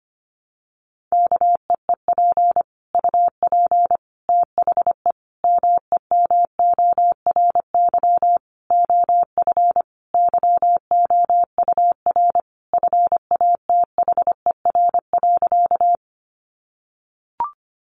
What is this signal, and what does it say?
Morse code, heard as a keyed tone, about 25 wpm, 700 Hz. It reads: KEEP UP THE MEMORY OF YOUR FATHER.
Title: Sentences from Top 1300 Words 25wpm